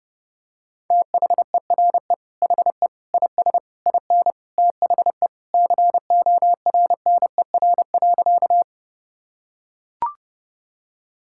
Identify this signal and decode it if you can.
Morse code, heard as a keyed tone, about 32 wpm, 700 Hz. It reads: THERE HE IS IN THE CORNER.
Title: Sentences from Top 1600 Words 30wpm